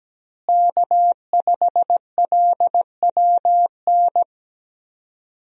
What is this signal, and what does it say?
Morse code, heard as a keyed tone, about 17 wpm, 700 Hz. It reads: K5LWN
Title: USA Call Signs - 5 Characters - Repeated 3x - 17wpm